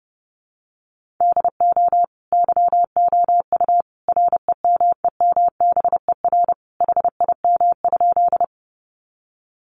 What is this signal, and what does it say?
Morse code, heard as a keyed tone, about 28 wpm, 700 Hz. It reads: DO YOU REMEMBER HIM?
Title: Sentences from Top 400 Words 30wpm